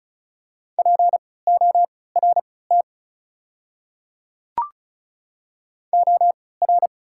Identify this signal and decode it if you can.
Morse code, heard as a keyed tone, about 34 wpm, spaced as for 20 wpm, 690 Hz. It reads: PORT OR
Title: 3x Character Spacing - Top 200 Words - 35wpm